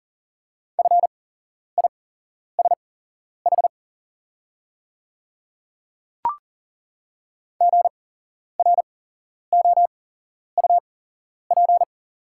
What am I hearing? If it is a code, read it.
Morse code, heard as a keyed tone, about 40 wpm, spaced as for 11 wpm, 690 Hz. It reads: FISH GROUP
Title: 8x Character Spacing - Top 300 Words - 40wpm